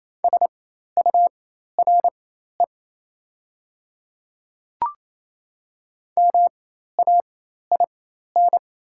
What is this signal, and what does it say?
Morse code, heard as a keyed tone, about 28 wpm, spaced as for 13 wpm, 700 Hz. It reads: SURE MAIN
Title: 4x Character Spacing - Top 500 Words - 28wpm